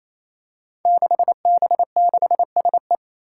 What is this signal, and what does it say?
Morse code, heard as a keyed tone, about 27 wpm, 700 Hz. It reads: 6B6SE
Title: Five Challenging Letter-Number Combinations 28wpm